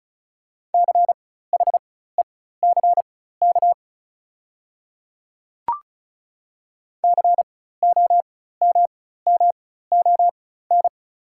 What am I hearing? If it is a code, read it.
Morse code, heard as a keyed tone, about 35 wpm, spaced as for 16 wpm, 690 Hz. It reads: CHECK COMMON
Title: 4x Character Spacing - Top 500 Words - 35wpm